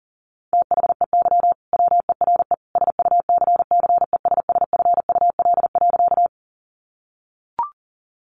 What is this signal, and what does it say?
Morse code, heard as a keyed tone, about 40 wpm, 700 Hz. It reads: THEY WERE SUCCESSFUL.
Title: Sentences from Top 900 Words 40wpm